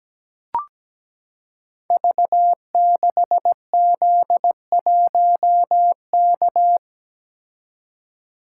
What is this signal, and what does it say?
Morse code, heard as a keyed tone, about 17 wpm, 700 Hz. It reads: V6Z1K
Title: Code Groups - 5 Letter:Numbers - 17wpm